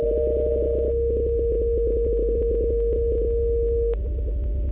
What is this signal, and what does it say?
PSK31, about 31.25 baud.
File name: luxorion / Radio / qso-bpsk31-g4ydo-s51ab- -1408z-14071-6khz